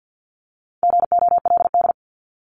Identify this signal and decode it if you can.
Morse code, heard as a keyed tone, about 50 wpm, 700 Hz. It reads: GOLD
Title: Multi-letter Phonemes Repeated 3x 50wpm